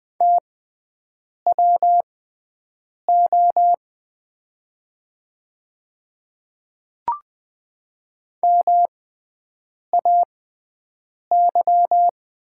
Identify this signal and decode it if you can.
Morse code, heard as a keyed tone, about 20 wpm, spaced as for 7 wpm, 700 Hz. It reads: TWO MAY
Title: 6x Character Spacing - Top 300 Words - 20wpm